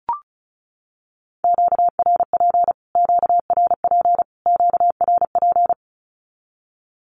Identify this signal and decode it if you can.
Morse code, heard as a keyed tone, about 34 wpm, 700 Hz. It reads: QRP QRP QRP
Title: Q-Codes Repeated 3x 35wpm